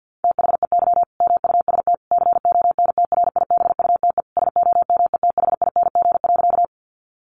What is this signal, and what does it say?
Morse code, heard as a keyed tone, about 50 wpm, 700 Hz. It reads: THEY MUST CONTRIBUTE SOMETHING.
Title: Sentences from Top 1400 Words 50wpm